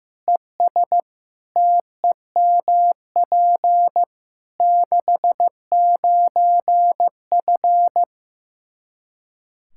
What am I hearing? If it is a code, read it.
Morse code, heard as a keyed tone, about 15 wpm, 700 Hz. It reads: ES TEMP 69F
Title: K1USN SST QSO Practice 15wpm